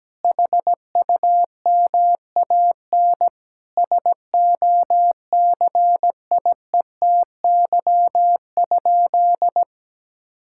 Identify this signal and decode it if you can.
Morse code, heard as a keyed tone, about 17 wpm, 700 Hz. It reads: HUMAN SOCIETY?
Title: Sentences from Top 1000 Words - Review Part 6 - 17wpm